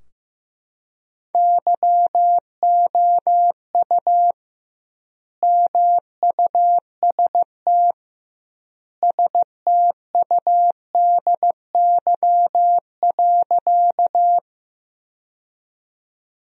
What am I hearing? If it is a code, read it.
Morse code, heard as a keyed tone, about 16 wpm, 700 Hz. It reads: YOU MUST STUDY.
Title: 2x Word-Spacing - Sentences from Top 300 Words - 15wpm